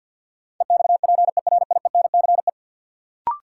Morse code, heard as a keyed tone, about 50 wpm, 700 Hz. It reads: EXPERIENCE
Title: Sentences from Top 600 Words - New Word Drill - 50wpm